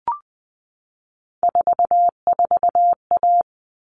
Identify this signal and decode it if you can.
Morse code, heard as a keyed tone, about 20 wpm, 700 Hz. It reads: 44A
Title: All Cut Numbers - 3-Digit 20wpm